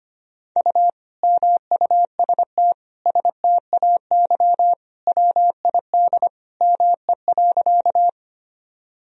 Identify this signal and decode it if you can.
Morse code, heard as a keyed tone, about 25 wpm, 700 Hz. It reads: U MUST STAY WID ME.
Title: Sentences from Top 500 Words and Abbreviations 25wpm